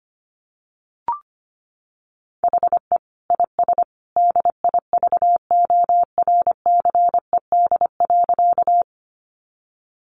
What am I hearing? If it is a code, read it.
Morse code, heard as a keyed tone, about 25 wpm, 700 Hz. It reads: HE IS DIVORCED.
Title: Additional Easy Phrases 25wpm